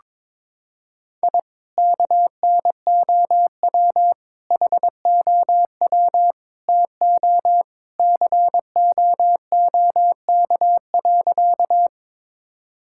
Morse code, heard as a keyed tone, about 22 wpm, 700 Hz. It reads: I KNOW HOW TO COOK.
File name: practice / 22wpm / Sentences with 3-Word Phrases - 22wpm